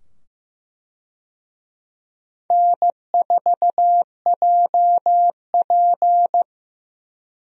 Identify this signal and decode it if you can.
Morse code, heard as a keyed tone, about 15 wpm, 700 Hz. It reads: N4JP
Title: USA Call Signs - 4 Characters - 15wpm